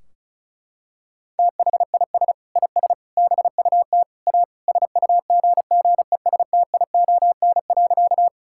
Morse code, heard as a keyed tone, about 36 wpm, 700 Hz. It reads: THIS IS BUT A SUGGESTION.
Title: Sentences from Top 1600 Words 35wpm